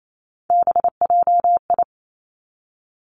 Morse code, heard as a keyed tone, about 28 wpm, 700 Hz. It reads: BJI
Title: Three Challenging Letter-Number Combinations 28wpm